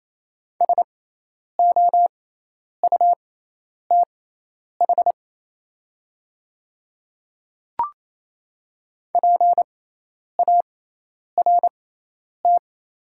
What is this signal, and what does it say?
Morse code, heard as a keyed tone, about 28 wpm, spaced as for 10 wpm, 700 Hz. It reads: SOUTH PART
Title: 6x Character Spacing - Top 500 Words - 28wpm